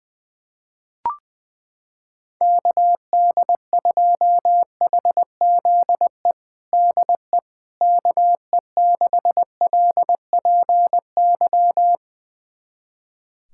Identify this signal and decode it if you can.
Morse code, heard as a keyed tone, about 20 wpm, 700 Hz. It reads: KD2HZE DE KE6LPY
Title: USA Callsign DE Callsign - 6 Characters - 20wpm